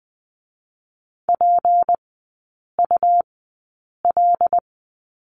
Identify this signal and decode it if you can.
Morse code, heard as a keyed tone, about 20 wpm, spaced as for 8 wpm, 690 Hz. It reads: PUL
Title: 166 Lesson 27e - Number 7 - Three Characters 20wpm